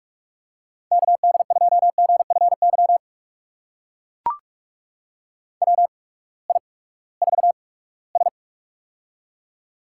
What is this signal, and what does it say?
Morse code, heard as a keyed tone, about 44 wpm, spaced as for 22 wpm, 700 Hz. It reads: KD2GFY WI4S
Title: USA Call Signs - 8x Character Spacing - 45wpm